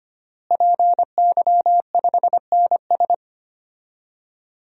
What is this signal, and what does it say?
Morse code, heard as a keyed tone, about 25 wpm, 700 Hz. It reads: PY5NS
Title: DXCC-100 Call Signs - 5 Characters - Repeated 3x - 25wpm